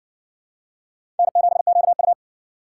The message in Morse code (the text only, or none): N7QU